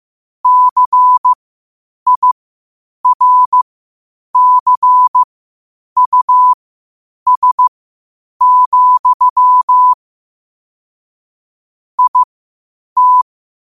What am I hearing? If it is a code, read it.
Morse code, heard as a keyed tone, about 15 wpm, spaced as for 8 wpm, 1000 Hz. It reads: CIRCUS, IT
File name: qotdfiles / 8wpm